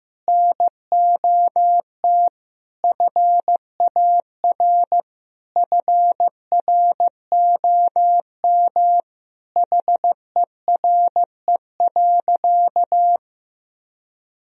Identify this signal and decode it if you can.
Morse code, heard as a keyed tone, about 15 wpm, 700 Hz. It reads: NOT FAR FROM HERE.
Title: Sentences from Top 1000 Words - Review Part 4 - 15wpm